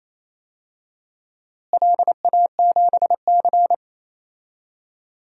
Morse code, heard as a keyed tone, about 28 wpm, 700 Hz. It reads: LA7C
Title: DXCC-100 Call Signs - 4 Characters - Repeated 2x - 28wpm